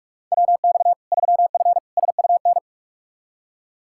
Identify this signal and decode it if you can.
Morse code, heard as a keyed tone, about 46 wpm, 700 Hz. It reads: WX 3F SUN
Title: Practice QSO - Exchange 2 - Info - 45wpm